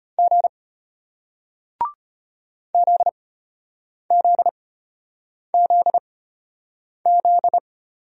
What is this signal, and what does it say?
Morse code, heard as a keyed tone, about 33 wpm, spaced as for 8 wpm, 700 Hz. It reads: G ZZZZ